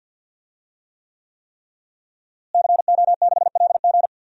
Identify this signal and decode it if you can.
Morse code, heard as a keyed tone, about 50 wpm, 700 Hz. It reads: CO6LG
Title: DXCC-100 Call Signs - 5 Characters - Repeated 3x - 50wpm